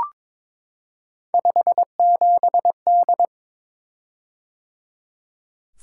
Morse code, heard as a keyed tone, about 22 wpm, 700 Hz. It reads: 57D
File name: practice / 22wpm / Three Letter-Number Combinations - 22wpm